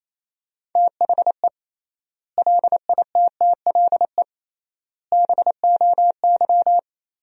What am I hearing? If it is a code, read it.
Morse code, heard as a keyed tone, about 28 wpm, 700 Hz. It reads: THE LITTLE BOY
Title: 3x Word-Spacing - Sentences from Top 500 Words - 28wpm